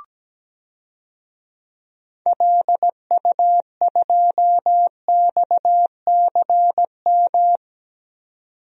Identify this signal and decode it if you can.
Morse code, heard as a keyed tone, about 17 wpm, 700 Hz. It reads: LU2XCM
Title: DXCC-100 Call Signs - 6 Characters - 17wpm